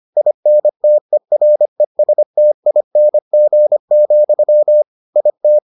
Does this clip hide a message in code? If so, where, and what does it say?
Yes: Morse code from 0.2 to 5.6 s, heard as a keyed tone, about 25 wpm, 600 Hz. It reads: INTERESTING, IT